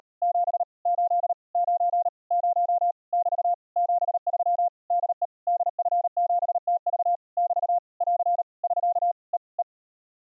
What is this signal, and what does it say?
Morse code, heard as a keyed tone, about 37 wpm, 700 Hz. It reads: 7 8 9 0 <BT> 73 DE DF7TV <BT> <AR> <SK> E E